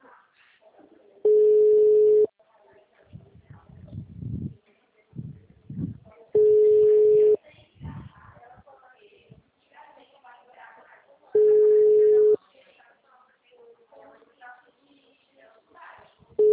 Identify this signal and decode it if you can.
Ringback tone (Europe, 425 Hz), with speech.